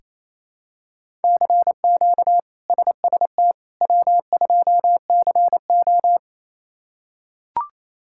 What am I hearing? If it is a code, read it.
Morse code, heard as a keyed tone, about 28 wpm, 700 Hz. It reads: CQ SST W2CO
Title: K1USN SST QSO Practice - 28wpm